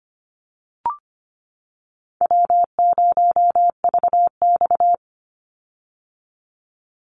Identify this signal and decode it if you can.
Morse code, heard as a keyed tone, about 25 wpm, 700 Hz. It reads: W0VX
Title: Easy Call Signs - 25wpm